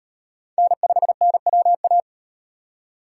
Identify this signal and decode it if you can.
Morse code, heard as a keyed tone, about 36 wpm, 700 Hz. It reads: N5NWA